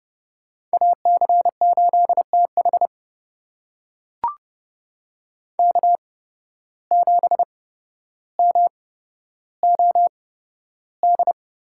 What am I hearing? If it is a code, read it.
Morse code, heard as a keyed tone, about 28 wpm, spaced as for 13 wpm, 700 Hz. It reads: AC8TH K7MOD
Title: USA Call Signs - 5 Characters - 8x Character Spacing - 30wpm